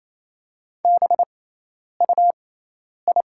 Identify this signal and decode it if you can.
Morse code, heard as a keyed tone, about 28 wpm, spaced as for 10 wpm, 700 Hz. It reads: BUI
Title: 6x Character Spacing - Top 300 Words - 28wpm